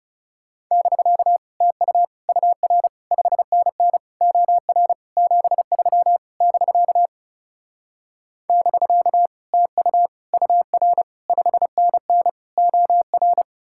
Morse code, heard as a keyed tone, about 40 wpm, 700 Hz. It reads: <BK> TU UR 5NN OR 73 <BK> <BK> TU UR 5NN OR